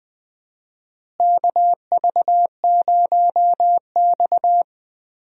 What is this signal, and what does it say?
Morse code, heard as a keyed tone, about 21 wpm, 700 Hz. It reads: KV0X
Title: USA Call Signs - 4 Characters - Repeated 2x - 20wpm